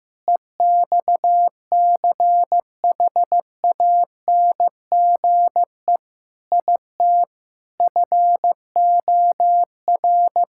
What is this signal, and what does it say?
Morse code, heard as a keyed tone, about 15 wpm, 700 Hz. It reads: EXCHANGE IT FOR
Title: Sentences from Top 1000 Words - Review Part 6 - 15wpm